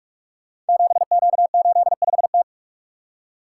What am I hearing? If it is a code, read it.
Morse code, heard as a keyed tone, about 44 wpm, 700 Hz. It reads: 7Q85T
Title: Code Groups - 5 Letter:Numbers - 45wpm